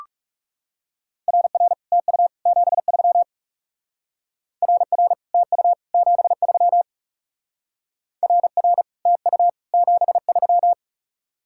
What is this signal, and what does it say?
Morse code, heard as a keyed tone, about 35 wpm, 700 Hz. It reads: RR TU 73 RR TU 73 RR TU 73
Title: Speed-Racing Common QSO Phrases - 30wpm